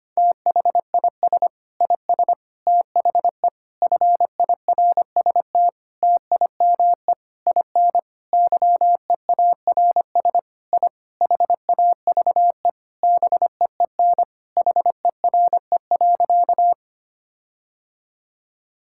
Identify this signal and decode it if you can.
Morse code, heard as a keyed tone, about 25 wpm, 700 Hz. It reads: THIS IS THE FIRST TIME IN YEARS I HAVE BEEN HERE.
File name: practice / 25wpm / Sentences with 4-Word Phrases - 25wpm